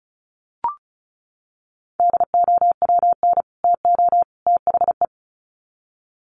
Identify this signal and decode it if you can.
Morse code, heard as a keyed tone, about 34 wpm, 700 Hz. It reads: DOWN TO THE
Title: Sets of 3 Words 35wpm